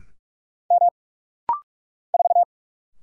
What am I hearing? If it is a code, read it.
Morse code, heard as a keyed tone, about 46 wpm, 700 Hz. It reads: M 4
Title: Single Letter-Number - Rapid-Fire 45wpm